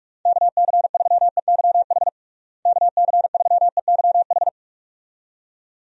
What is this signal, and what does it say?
Morse code, heard as a keyed tone, about 46 wpm, 700 Hz. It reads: KC3EYH KC3EYH
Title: USA Call Signs - 6 Characters - Repeated 2x - 45wpm